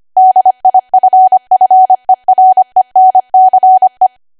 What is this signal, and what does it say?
Morse code, heard as a keyed tone, about 25 wpm, 750 Hz. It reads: DIFFERENCE
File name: Archive / 25 WPM / 25WPM